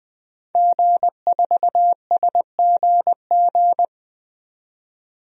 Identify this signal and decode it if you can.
Morse code, heard as a keyed tone, about 20 wpm, 700 Hz. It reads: G4SGG
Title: DXCC-100 Call Signs - 5 Characters - 20wpm